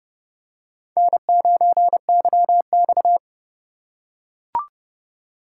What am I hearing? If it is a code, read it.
Morse code, heard as a keyed tone, about 30 wpm, 700 Hz. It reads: N9YX